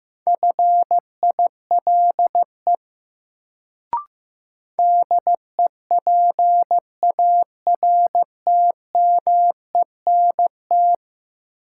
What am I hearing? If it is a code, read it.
Morse code, heard as a keyed tone, about 15 wpm, 700 Hz. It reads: FILE DEPARTMENT
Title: Sentences from Top 1200 Words - New Word Drill Alt - 15wpm